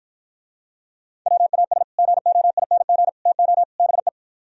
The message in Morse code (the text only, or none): WAS GOING TO BE